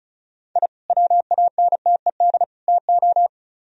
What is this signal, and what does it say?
Morse code, heard as a keyed tone, about 43 wpm, 700 Hz. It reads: I WANTED TO
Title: Speed-Racing - Sets of 3 Words 35wpm